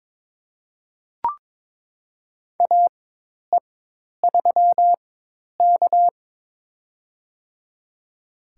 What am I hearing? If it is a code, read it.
Morse code, heard as a keyed tone, about 21 wpm, spaced as for 10 wpm, 700 Hz. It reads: AE3K
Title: USA Call Signs - 4x Character Spacing - 22wpm